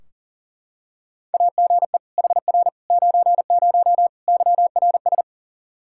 Morse code, heard as a keyed tone, about 40 wpm, 700 Hz. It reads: AGE HR 90 YRS
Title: Practice QSO - Exchange 3 - More Info - 40wpm